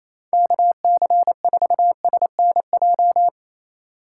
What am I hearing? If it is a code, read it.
Morse code, heard as a keyed tone, about 28 wpm, 700 Hz. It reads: KC4SNJ
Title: USA Call Signs - 6 Characters - 3x Character Spacing - 28wpm